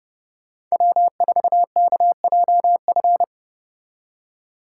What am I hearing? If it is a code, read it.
Morse code, heard as a keyed tone, about 35 wpm, 700 Hz. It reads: W4KJF
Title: Speed Racing - USA Call Signs 30wpm